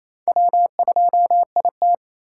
Morse code, heard as a keyed tone, about 35 wpm, 700 Hz. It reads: W2IT